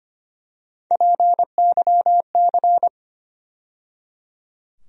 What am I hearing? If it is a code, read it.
Morse code, heard as a keyed tone, about 25 wpm, 700 Hz. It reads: PYC
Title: USA Call Sign Postfixes - 3 Letters 25wpm